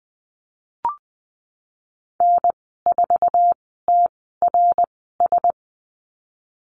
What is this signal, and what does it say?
Morse code, heard as a keyed tone, about 20 wpm, spaced as for 14 wpm, 700 Hz. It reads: N4TRS
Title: 069 Lesson 12g - Letter d - Call Signs 20wpm